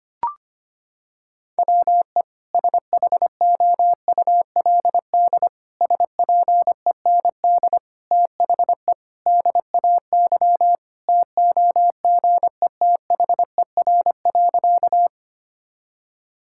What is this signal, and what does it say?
Morse code, heard as a keyed tone, about 25 wpm, 700 Hz. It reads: WE SHOULD SPEND THE DAY TOGETHER.
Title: Sentences from Top 1000 Words - Review Part 5 - 25wpm